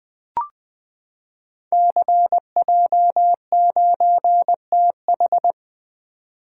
Morse code, heard as a keyed tone, about 20 wpm, 700 Hz. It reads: CJ9TH
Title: Code Groups - 5 Letter:Numbers - 20wpm